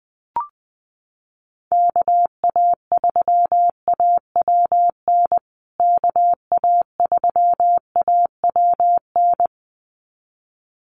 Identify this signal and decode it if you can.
Morse code, heard as a keyed tone, about 19 wpm, 700 Hz. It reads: KA3AWN KA3AWN
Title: USA Call Signs Repeated 2x 20wpm